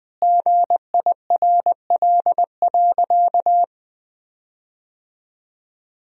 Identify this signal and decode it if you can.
Morse code, heard as a keyed tone, about 21 wpm, 700 Hz. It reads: GIRL.
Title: 4x Word-Spacing - Sentences from Top 400 Words - 20wpm